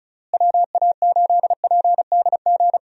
Morse code, heard as a keyed tone, about 34 wpm, 700 Hz. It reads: WA8PDG
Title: USA Call Signs - 6 Characters - Repeated 3x - 35wpm